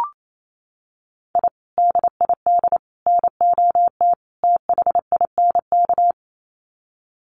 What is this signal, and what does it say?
Morse code, heard as a keyed tone, about 28 wpm, 700 Hz. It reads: I DID NOT THINK
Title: Sets of 4 Words 28wpm